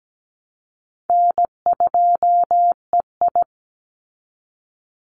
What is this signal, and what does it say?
Morse code, heard as a keyed tone, about 17 wpm, 700 Hz. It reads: N2EI
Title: Easy Call Signs Repeated 3x - 17wpm